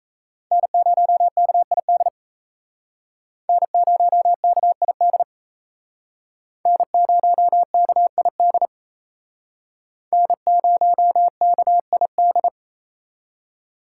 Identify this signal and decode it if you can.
Morse code, heard as a keyed tone, about 36 wpm, 700 Hz. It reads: N0KID N0KID N0KID N0KID